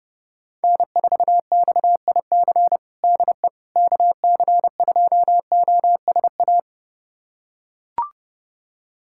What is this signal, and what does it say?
Morse code, heard as a keyed tone, about 30 wpm, 700 Hz. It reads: N4XIC DE KC2OSA
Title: USA Callsign DE Callsign 30wpm